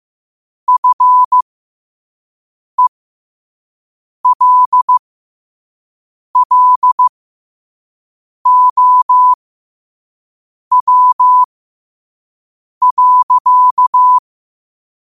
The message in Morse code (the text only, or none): FELLOW.